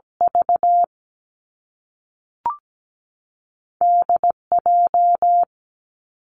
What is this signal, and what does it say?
Morse code, heard as a keyed tone, about 17 wpm, 700 Hz. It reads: V D J
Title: Two Challenging Letter-Number Combinations 17wpm